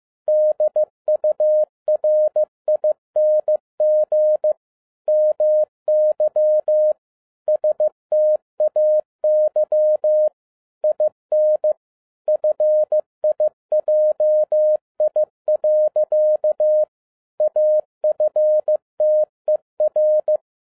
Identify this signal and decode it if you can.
Morse code, heard as a keyed tone, about 15 wpm, 610 Hz. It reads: DURING MY STAY IN FIJI. AFTER
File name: 2011-10-mp3 / n07-A DOT 15wpm.mp3